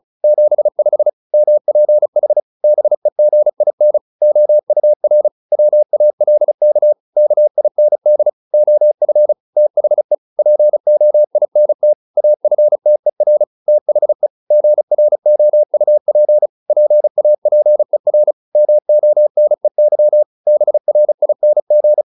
Morse code, heard as a keyed tone, about 34 wpm, 600 Hz. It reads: 75 MPH BEGIN OUR WALK KIND OF THE POINT AFTER THE GROUP PAPER MONEY BRING